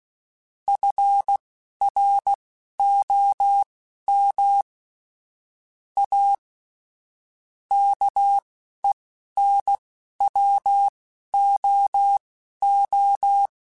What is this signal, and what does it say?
Morse code, heard as a keyed tone, about 16 wpm, spaced as for 11 wpm, 800 Hz. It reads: FROM A KENWOO